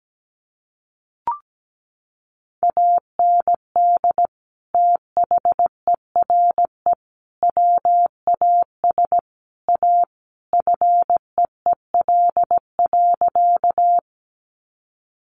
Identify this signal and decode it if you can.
Morse code, heard as a keyed tone, about 17 wpm, 700 Hz. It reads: AND THERE WAS A FEEL.
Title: A Sound Of Thunder 17wpm